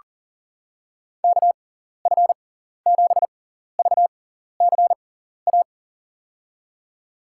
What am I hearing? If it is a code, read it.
Morse code, heard as a keyed tone, about 40 wpm, spaced as for 14 wpm, 700 Hz. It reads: KF7VCA
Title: USA Call Signs - 6 Characters - 6x Character Spacing - 40wpm